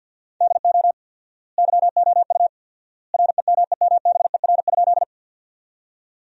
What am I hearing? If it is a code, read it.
Morse code, heard as a keyed tone, about 50 wpm, 700 Hz. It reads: DO YOU REMEMBER?
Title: 4x Word-Spacing - Sentences from Top 400 Words - 50wpm